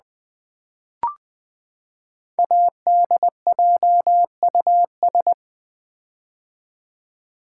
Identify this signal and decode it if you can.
Morse code, heard as a keyed tone, about 20 wpm, 700 Hz. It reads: ADJUS
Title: Build A Word - 5 Letters At A Time - 10-Letter Words - 20wpm